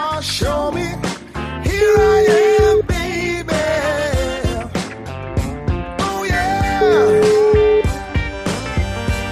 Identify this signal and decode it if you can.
Ringback tone (Europe, 425 Hz), with noise.